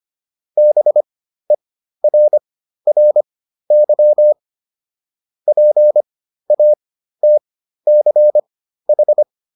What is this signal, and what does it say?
Morse code, heard as a keyed tone, about 25 wpm, spaced as for 13 wpm, 600 Hz. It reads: BERRY PATCH